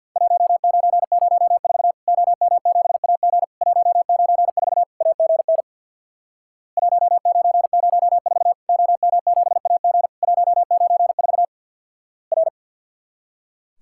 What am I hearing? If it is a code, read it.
Morse code, heard as a keyed tone, about 50 wpm, 700 Hz. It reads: 1904 OM7AG 194 AGN 1904 OM7AG 194 R
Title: Worked All Europe DX Contest - QTC Exchange with Repeats - 50wpm